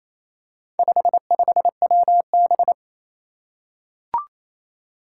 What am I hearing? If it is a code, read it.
Morse code, heard as a keyed tone, about 28 wpm, 700 Hz. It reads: 55WB
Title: Four Challenging Letter-Number Combinations 28wpm